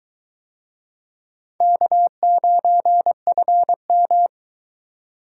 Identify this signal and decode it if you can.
Morse code, heard as a keyed tone, about 21 wpm, 700 Hz. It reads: K9FM